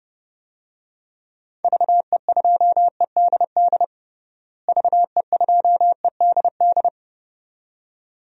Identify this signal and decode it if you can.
Morse code, heard as a keyed tone, about 30 wpm, 700 Hz. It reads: VE2EDD VE2EDD
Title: DXCC-100 Call Signs - 6 Characters - Repeated 2x - 30wpm